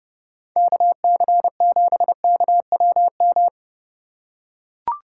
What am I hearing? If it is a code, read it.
Morse code, heard as a keyed tone, about 30 wpm, 700 Hz. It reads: KC7KWM